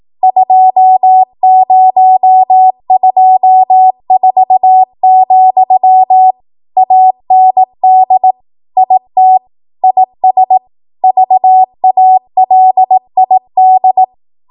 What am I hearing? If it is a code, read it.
Morse code, heard as a keyed tone, about 18 wpm, 750 Hz. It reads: 2024, AND IT IS VALID